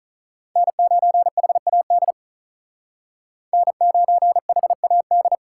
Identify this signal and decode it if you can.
Morse code, heard as a keyed tone, about 47 wpm, 700 Hz. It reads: N9HAD N9HAD